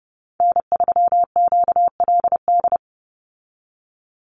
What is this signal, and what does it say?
Morse code, heard as a keyed tone, about 30 wpm, 700 Hz. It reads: N3QLD
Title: USA Call Signs 30wpm